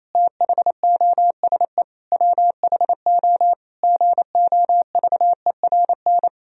Morse code, heard as a keyed tone, about 27 wpm, 700 Hz. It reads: THOSE WHO GOVERN
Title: Sentences from Top 1000 Words - Review Part 2 - 28wpm